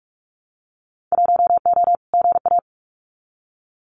Morse code, heard as a keyed tone, about 44 wpm, 700 Hz. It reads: 1O GA
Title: Winter Field Day Exchanges 45wpm